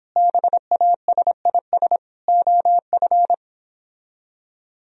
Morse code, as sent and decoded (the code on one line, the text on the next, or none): -... .- ... .. ... / --- ..-.
BASIS OF